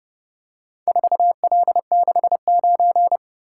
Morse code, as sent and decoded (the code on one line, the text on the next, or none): ....- .-.. -.... ----.
4L69